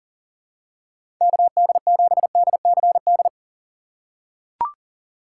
Morse code, heard as a keyed tone, about 37 wpm, 700 Hz. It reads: KD7DCD
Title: USA Call Signs - 6 Characters - 2x Character Spacing - 40wpm